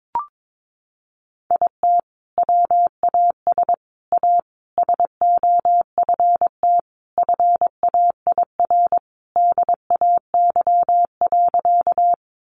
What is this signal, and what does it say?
Morse code, heard as a keyed tone, about 22 wpm, 700 Hz. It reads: IT WAS A SOFT FAIR DAY.
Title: Sentences from Top 1500 Words 22wpm